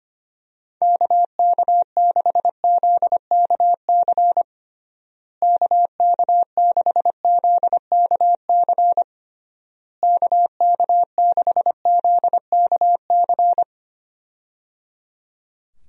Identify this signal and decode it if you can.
Morse code, heard as a keyed tone, about 25 wpm, 700 Hz. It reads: KK6ZKC KK6ZKC KK6ZKC
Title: USA Call Signs - 6 Characters - Repeated 3x - 25wpm